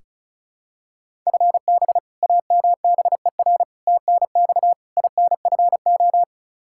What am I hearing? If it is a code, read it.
Morse code, heard as a keyed tone, about 36 wpm, 700 Hz. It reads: FB AMBER TNX INFO
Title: Practice QSO - Exchange 3 - More Info - 35wpm